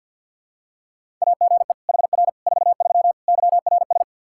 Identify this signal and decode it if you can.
Morse code, heard as a keyed tone, about 50 wpm, 700 Hz. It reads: AGE HR 43 YRS
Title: K1USN SST QSO Practice 50wpm